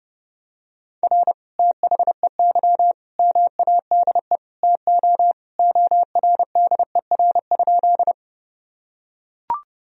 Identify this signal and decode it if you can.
Morse code, heard as a keyed tone, about 32 wpm, 700 Hz. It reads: R THEY MADE TO ORDER?
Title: Sentences from Top 400 Words and Abbreviations 30wpm